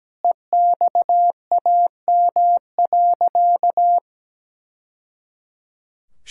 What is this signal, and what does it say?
Morse code, heard as a keyed tone, about 17 wpm, 700 Hz. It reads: EXAM.
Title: Sentences from Top 1000 Words - Review Part 5 - 17wpm